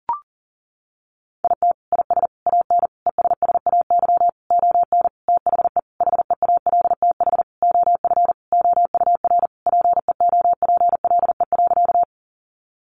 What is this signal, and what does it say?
Morse code, heard as a keyed tone, about 40 wpm, 700 Hz. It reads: IT IS AN ESSAY ON THE HEALTH OF OUR PEOPLE.
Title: Sentences from Top 1500 Words 40wpm